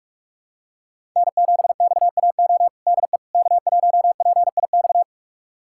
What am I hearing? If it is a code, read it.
Morse code, heard as a keyed tone, about 45 wpm, 700 Hz. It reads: N7XAO DE K1PIX
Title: USA Callsign DE Callsign - 5 Characters - 45wpm